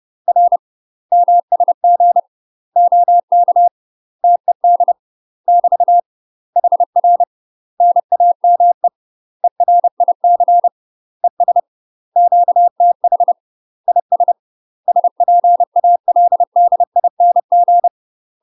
Morse code, heard as a keyed tone, about 30 wpm, 700 Hz. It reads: R MSG OK TED <BT> HR NAME ERIC ES QTH IS SPALDING
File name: UK MASTER QSOs TRAINING / 30WPM / UK MASTER QSOs 0041